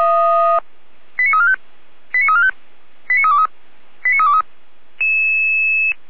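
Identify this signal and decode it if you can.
Five-tone selective call (ZVEI), heard as quick runs of five tones, tones of about 80 ms. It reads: 89345 89345 89323 89323
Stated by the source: ZVEI five-tone call 89305 received on an SDR, channel 455, logged by feuerwehr-orlamuende.org fire brigade.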